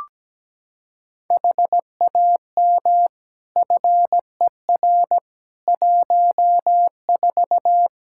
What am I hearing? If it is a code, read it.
Morse code, heard as a keyed tone, about 17 wpm, 700 Hz. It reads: HAM FER 14
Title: K1USN SST QSO Practice 17wpm